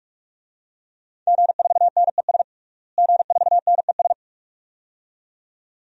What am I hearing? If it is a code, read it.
Morse code, heard as a keyed tone, about 41 wpm, 700 Hz. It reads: G4NES G4NES
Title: DXCC-100 Call Signs - 5 Characters - Repeated 2x - 45wpm